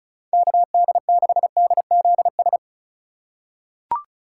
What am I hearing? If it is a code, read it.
Morse code, heard as a keyed tone, about 36 wpm, 700 Hz. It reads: KD6DZS
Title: USA Call Signs - 3x Character Spacing - 35wpm